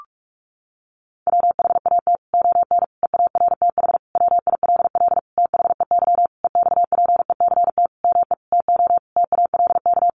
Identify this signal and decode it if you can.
Morse code, heard as a keyed tone, about 44 wpm, 700 Hz. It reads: WHAT ON EARTH WILL THEY EXPECT ME TO TALK
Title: Sentences from Top 1100 Words 45wpm